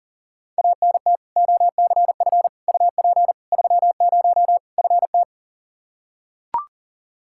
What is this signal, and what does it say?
Morse code, heard as a keyed tone, about 40 wpm, 700 Hz. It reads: ANT OCF UP 30 FT
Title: Practice QSO - Exchange 2 - Info - 40wpm